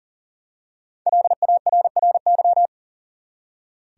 Morse code, heard as a keyed tone, about 40 wpm, 700 Hz. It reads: LARRY
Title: USA Common Names - 40wpm